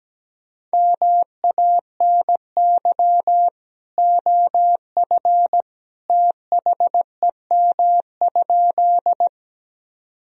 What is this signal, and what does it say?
Morse code, heard as a keyed tone, about 17 wpm, 700 Hz. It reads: MANY OF THEM?
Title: Sentences from Top 1000 Words - Review Part 1 - 17wpm